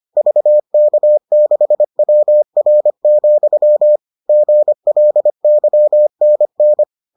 Morse code, heard as a keyed tone, about 25 wpm, 600 Hz. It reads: VK6WR, GLYNN